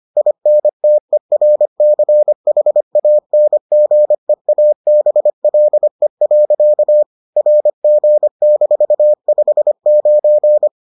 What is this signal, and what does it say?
Morse code, heard as a keyed tone, about 25 wpm, 600 Hz. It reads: INTERCHANGEABLE. RG-59